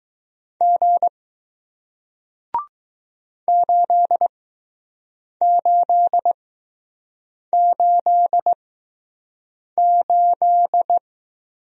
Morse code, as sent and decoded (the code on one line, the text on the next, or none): --. / ---.. ---.. ---.. ---..
G 8888